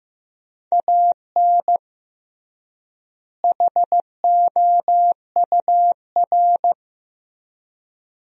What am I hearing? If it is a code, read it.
Morse code, heard as a keyed tone, about 15 wpm, 700 Hz. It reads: AN HOUR
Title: 3x Word-Spacing - Sentences from Top 400 Words - 15wpm